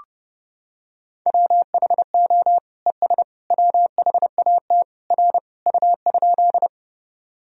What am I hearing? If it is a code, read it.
Morse code, heard as a keyed tone, about 32 wpm, 700 Hz. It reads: WHO ES WHAT R U?
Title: Sentences from Top 100 Words and Abbreviations 30wpm